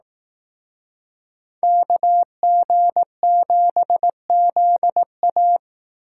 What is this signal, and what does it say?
Morse code, heard as a keyed tone, about 18 wpm, 700 Hz. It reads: KG7ZA
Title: Speed Racing - USA Call Signs 15wpm